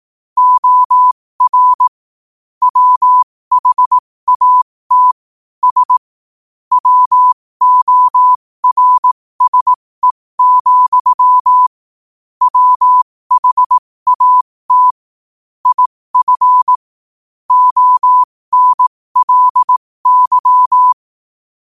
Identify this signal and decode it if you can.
Morse code, heard as a keyed tone, about 18 wpm, spaced as for 15 wpm, 1000 Hz. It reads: OR WHAT S WORSE, WHAT IF ONLY